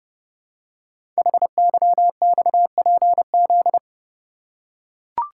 Morse code, heard as a keyed tone, about 30 wpm, 700 Hz. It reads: HYXPZ
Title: Code Groups - 5 Letters - 30wpm